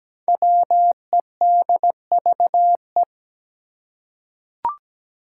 Morse code, heard as a keyed tone, about 17 wpm, 700 Hz. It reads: WEDVE
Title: Five Challenging Letter-Number Combinations 17wpm